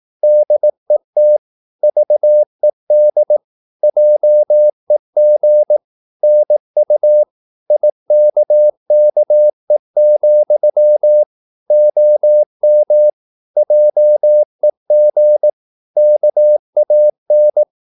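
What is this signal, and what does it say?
Morse code, heard as a keyed tone, about 18 wpm, 600 Hz. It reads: DET VED JEG NU IKKE, OM JEG KAN
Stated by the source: Slow (QRS) Morse practice at 18 wpm, CW med Gnister.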